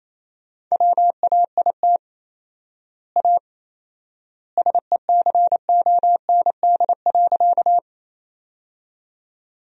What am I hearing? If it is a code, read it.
Morse code, heard as a keyed tone, about 28 wpm, 700 Hz. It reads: WAIT A SECOND.
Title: 4x Word-Spacing - Sentences from Top 500 Words - 28wpm